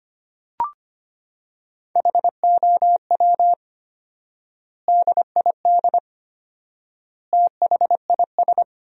Morse code, heard as a keyed tone, about 25 wpm, 700 Hz. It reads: HOW DID THIS
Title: 4x Word-Spacing - Sentences from Top 400 Words - 25wpm